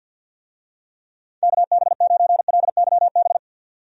Morse code, heard as a keyed tone, about 46 wpm, 700 Hz. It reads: KB9LYB
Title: USA Call Signs - 6 Characters - 50wpm